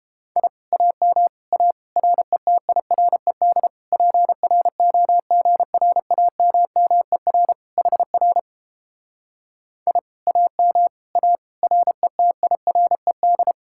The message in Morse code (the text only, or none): I AM A RETIRED PROGRAMMER HR I AM A RETIRED